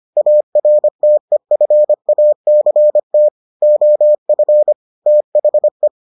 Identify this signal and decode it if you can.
Morse code, heard as a keyed tone, about 25 wpm, 600 Hz. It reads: ARTEFACT OF THE